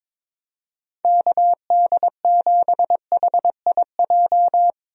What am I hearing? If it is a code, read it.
Morse code, heard as a keyed tone, about 22 wpm, 700 Hz. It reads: KD7HIJ
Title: USA Call Signs - 6 Characters - Repeated 2x - 22wpm